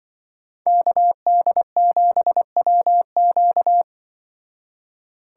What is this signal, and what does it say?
Morse code, heard as a keyed tone, about 25 wpm, 700 Hz. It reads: KD7WQ
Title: Speed Racing - USA 2x2 Call Signs 20wpm